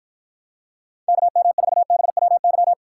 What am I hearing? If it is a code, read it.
Morse code, heard as a keyed tone, about 53 wpm, 700 Hz. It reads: KM4BWY